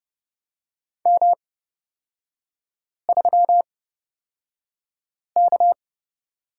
Morse code, heard as a keyed tone, about 29 wpm, spaced as for 5 wpm, 700 Hz. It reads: M3K
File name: Lesson 36e - BT - Three Characters 30wpm